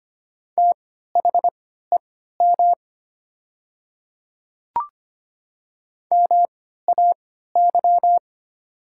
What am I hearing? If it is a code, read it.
Morse code, heard as a keyed tone, about 25 wpm, spaced as for 14 wpm, 700 Hz. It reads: THEM MAY
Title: 3x Character Spacing - Top 100 Words - 25wpm